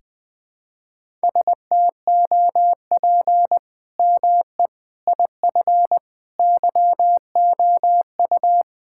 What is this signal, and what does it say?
Morse code, heard as a keyed tone, about 21 wpm, 700 Hz. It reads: STOP ME IF YOU
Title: Sentences with 8-Word Phrases - 20wpm